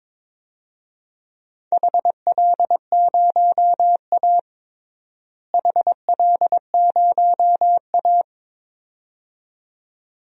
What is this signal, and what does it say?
Morse code, heard as a keyed tone, about 21 wpm, 700 Hz. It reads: HL0A HL0A
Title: DXCC-100 Call Signs - 4 Characters - Repeated 2x - 22wpm